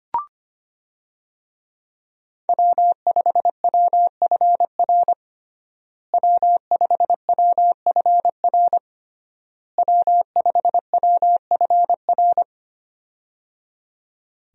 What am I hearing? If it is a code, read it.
Morse code, heard as a keyed tone, about 25 wpm, 700 Hz. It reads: W5WFR W5WFR W5WFR
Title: USA Call Signs - 5 Characters - Repeated 3x - 25wpm